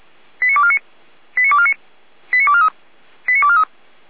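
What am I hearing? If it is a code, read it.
Five-tone selective call (ZVEI), heard as quick runs of five tones, tones of about 70 ms. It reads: 89249 89249 89243 89243
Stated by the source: ZVEI five-tone selective call received on an SDR.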